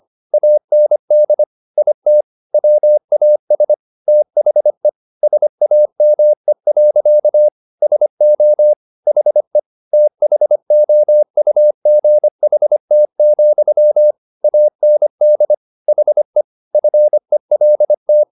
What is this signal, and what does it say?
Morse code, heard as a keyed tone, about 25 wpm, 600 Hz. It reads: AND IT WAS THE SAME. SO HE THOUGHT, AND HE FELT